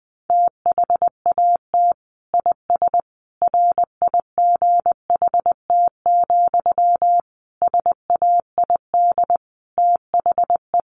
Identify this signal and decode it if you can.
Morse code, heard as a keyed tone, about 20 wpm, 700 Hz. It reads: THAT IS RIGHT, SAID THE
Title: Sentences from Top 200 Words 20wpm